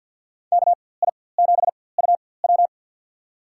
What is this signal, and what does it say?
Morse code, heard as a keyed tone, about 52 wpm, spaced as for 24 wpm, 700 Hz. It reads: KI7UW
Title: USA Call Signs - 5 Characters - 4x Character Spacing - 50wpm